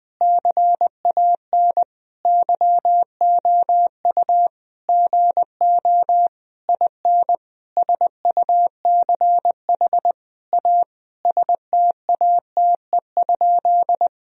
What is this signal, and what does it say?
Morse code, heard as a keyed tone, about 20 wpm, 700 Hz. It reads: CAN YOU GO IN SUCH A STATE?
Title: Sentences from Top 1000 Words - Review Part 2 - 20wpm